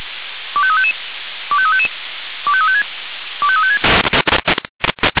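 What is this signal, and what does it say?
Five-tone selective call (ZVEI), heard as quick runs of five tones, tones of about 70 ms. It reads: 25355 25355 25356 25356
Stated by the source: ZVEI five-tone selective call 25345 received with an SDR.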